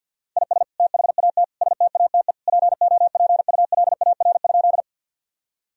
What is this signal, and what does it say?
Morse code, heard as a keyed tone, about 50 wpm, 700 Hz. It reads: IS THAT STATE POPULAR?
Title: Sentences from Top 1000 Words - Review Part 3 - 50wpm